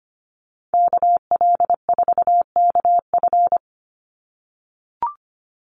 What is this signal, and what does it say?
Morse code, heard as a keyed tone, about 25 wpm, 700 Hz. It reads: KL4KF
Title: USA Call Signs 25wpm